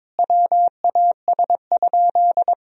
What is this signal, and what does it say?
Morse code, heard as a keyed tone, about 21 wpm, 700 Hz. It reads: WAS?
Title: Sentences from Top 200 Words and Abbreviations 22wpm